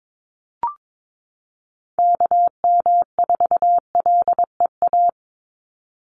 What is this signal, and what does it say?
Morse code, heard as a keyed tone, about 22 wpm, 700 Hz. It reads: KM4LEA
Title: USA Call Signs 22wpm